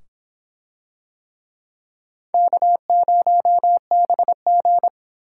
Morse code, heard as a keyed tone, about 24 wpm, 700 Hz. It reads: K0BG